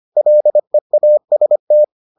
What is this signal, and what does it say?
Morse code, heard as a keyed tone, about 25 wpm, 600 Hz. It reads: LEAST